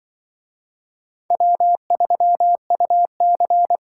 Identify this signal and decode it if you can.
Morse code, heard as a keyed tone, about 25 wpm, 700 Hz. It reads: W3UC
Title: Speed Racing - USA Call Signs 20wpm